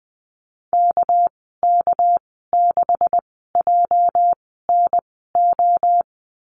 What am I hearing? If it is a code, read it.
Morse code, heard as a keyed tone, about 20 wpm, spaced as for 14 wpm, 700 Hz. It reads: KK6JNO
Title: 219 Lesson 35g - Number 0 - Call Signs 20wpm